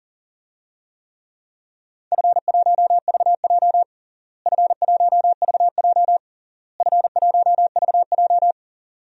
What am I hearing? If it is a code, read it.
Morse code, heard as a keyed tone, about 37 wpm, 700 Hz. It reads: F1VJ F1VJ F1VJ
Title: DXCC-100 Call Signs - 4 Characters - Repeated 3x - 40wpm